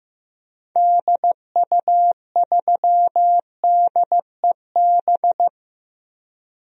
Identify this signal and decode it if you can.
Morse code, heard as a keyed tone, about 15 wpm, 700 Hz. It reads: DU3DEB
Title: DXCC-100 Call Signs - 6 Characters - Repeated 3x - 15wpm